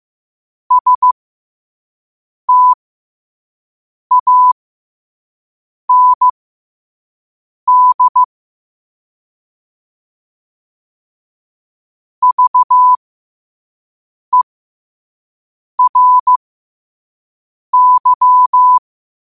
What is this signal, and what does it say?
Morse code, heard as a keyed tone, about 15 wpm, spaced as for 5 wpm, 1000 Hz. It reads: STAND VERY
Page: Quotes for Thu, 14 Aug 2025 in Morse Code at 5 words per minute.